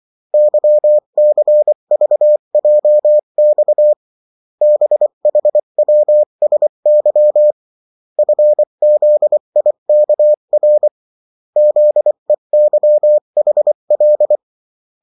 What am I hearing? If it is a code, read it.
Morse code, heard as a keyed tone, about 24 wpm, 600 Hz. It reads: YCVJX BHWSY FZIKR ZEYHL
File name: morse mp3 / revision9-vitesse 22 mots